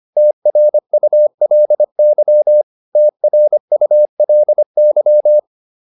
Morse code, heard as a keyed tone, about 25 wpm, 600 Hz. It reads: TRULY TRULY